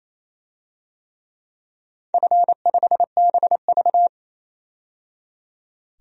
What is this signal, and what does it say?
Morse code, heard as a keyed tone, about 27 wpm, 700 Hz. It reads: F5BV
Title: DXCC-100 Call Signs - 4 Characters - 28wpm